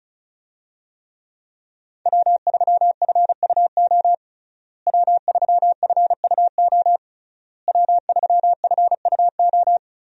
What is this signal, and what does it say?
Morse code, heard as a keyed tone, about 34 wpm, 700 Hz. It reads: W3FUO W3FUO W3FUO
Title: USA Call Signs - 5 Characters - Repeated 3x - 35wpm